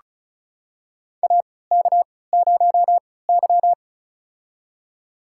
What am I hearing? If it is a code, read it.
Morse code, heard as a keyed tone, about 36 wpm, spaced as for 20 wpm, 700 Hz. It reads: AK0Y
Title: USA Call Signs - 4 Characters - 3x Character Spacing - 35wpm